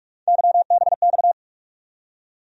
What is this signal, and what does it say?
Morse code, heard as a keyed tone, about 46 wpm, 700 Hz. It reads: YBX
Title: USA Call Sign Postfixes - 3 Letters 45wpm